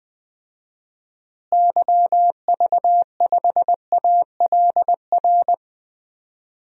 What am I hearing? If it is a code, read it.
Morse code, heard as a keyed tone, about 20 wpm, 700 Hz. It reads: YV5ALR
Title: DXCC-100 Call Signs - 6 Characters - Repeated 2x - 20wpm